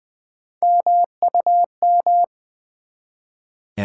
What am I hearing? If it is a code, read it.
Morse code, heard as a keyed tone, about 20 wpm, 700 Hz. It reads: MUM
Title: Build A Word - 1 Letter At A Time - 3-Letter Words - 20wpm